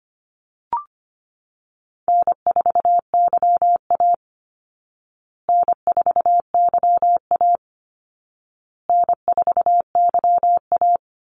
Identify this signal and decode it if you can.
Morse code, heard as a keyed tone, about 25 wpm, 700 Hz. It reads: N4YA N4YA N4YA
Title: Easy Call Signs Repeated 3x - 25wpm